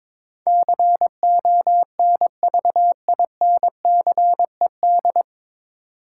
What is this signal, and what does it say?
Morse code, heard as a keyed tone, about 22 wpm, 700 Hz. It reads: CONVINCED
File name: 9 Letter Words - First 3 Letters Given - 22wpm